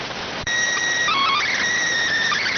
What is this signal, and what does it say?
SSTV (slow-scan television), PD 290.